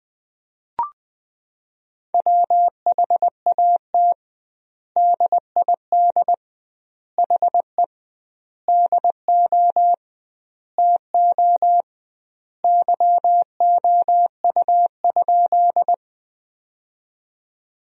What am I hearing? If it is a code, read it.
Morse code, heard as a keyed tone, about 21 wpm, 700 Hz. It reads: WHAT DID HE DO TO YOU?
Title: 2x Word-Spacing - Sentences from Top 100 Words - 20wpm